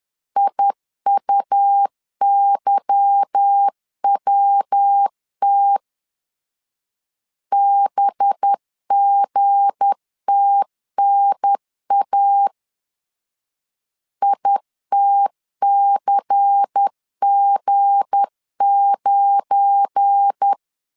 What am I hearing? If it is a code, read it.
Morse code, heard as a keyed tone, about 10 wpm, 800 Hz. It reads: IUYWT BGTNA ITCG9
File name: LEZIONI / 8 YJ45 / 8 Y 10wpm 2615